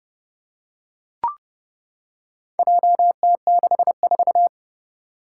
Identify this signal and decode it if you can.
Morse code, heard as a keyed tone, about 31 wpm, 700 Hz. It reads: JT64
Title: Four Challenging Letter-Number Combinations 30wpm